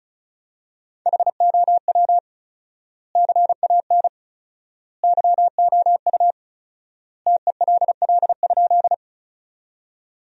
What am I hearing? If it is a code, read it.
Morse code, heard as a keyed tone, about 36 wpm, 700 Hz. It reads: HOW CAN YOU TELL?
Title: 4x Word-Spacing - Sentences from Top 200 Words - 35wpm